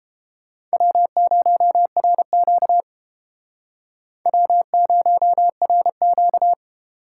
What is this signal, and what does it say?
Morse code, heard as a keyed tone, about 30 wpm, 700 Hz. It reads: W0RQ W0RQ